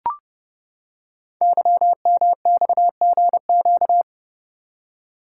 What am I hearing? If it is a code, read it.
Morse code, heard as a keyed tone, about 30 wpm, 700 Hz. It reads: YMXGQ
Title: Code Groups - 5 Letters - 30wpm